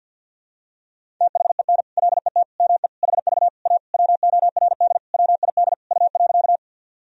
Morse code, heard as a keyed tone, about 46 wpm, 700 Hz. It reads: THEN LET ME HV A WORD WID U.
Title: Sentences from Top 300 Words and Abbreviations 50wpm